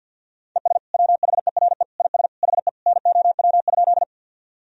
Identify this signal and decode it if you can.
Morse code, heard as a keyed tone, about 46 wpm, 700 Hz. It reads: ES WHERE IS HE NOW?
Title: Sentences from Top 200 Words and Abbreviations 50wpm